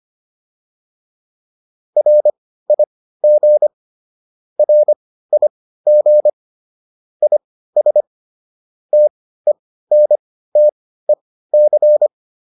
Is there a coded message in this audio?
Yes: Morse code from 2.0 to 12.1 s, heard as a keyed tone, about 25 wpm, spaced as for 15 wpm, 600 Hz. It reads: RIG RIG IS TENTEC